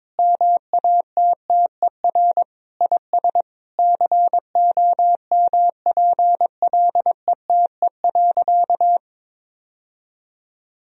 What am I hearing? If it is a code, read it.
Morse code, heard as a keyed tone, about 22 wpm, 700 Hz. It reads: MATTER IS COMPLETE.
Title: Sentences from Top 1000 Words - Review Part 2 - 22wpm